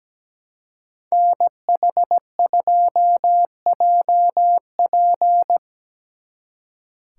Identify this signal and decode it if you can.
Morse code, heard as a keyed tone, about 19 wpm, 700 Hz. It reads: NH2JP
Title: Speed Racing - USA 2x2 Call Signs 17wpm